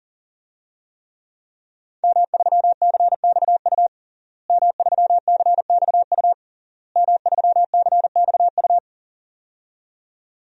Morse code, heard as a keyed tone, about 40 wpm, 700 Hz. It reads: M3CXU M3CXU M3CXU
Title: DXCC-100 Call Signs - 5 Characters - Repeated 3x - 40wpm